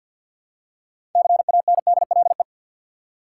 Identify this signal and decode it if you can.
Morse code, heard as a keyed tone, about 52 wpm, 700 Hz. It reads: CANDLE
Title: Six Letter Words - 50wpm